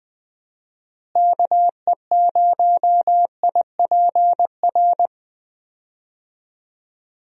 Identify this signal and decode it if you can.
Morse code, heard as a keyed tone, about 20 wpm, 700 Hz. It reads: KE0IPR
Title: USA Call Signs - 6 Characters - Repeated 3x - 20wpm